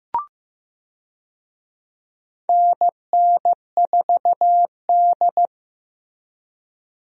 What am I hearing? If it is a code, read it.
Morse code, heard as a keyed tone, about 15 wpm, 700 Hz. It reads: NN4D
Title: USA Call Signs - 4 Characters - Repeated 3x - 15wpm